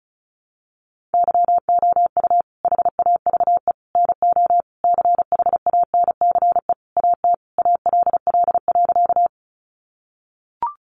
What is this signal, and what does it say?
Morse code, heard as a keyed tone, about 34 wpm, 700 Hz. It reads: YOU HAVE NO CHANCE AT ALL.
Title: Sentences from Top 800 Words 35wpm